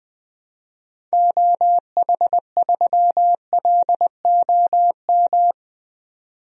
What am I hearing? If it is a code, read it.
Morse code, heard as a keyed tone, about 20 wpm, 700 Hz. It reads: OH3LOM
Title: DXCC-100 Call Signs - 6 Characters - 20wpm